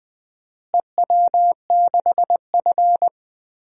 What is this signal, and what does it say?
Morse code, heard as a keyed tone, about 20 wpm, 700 Hz. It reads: EW6F
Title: DXCC-100 Call Signs - 4 Characters - Repeated 2x - 20wpm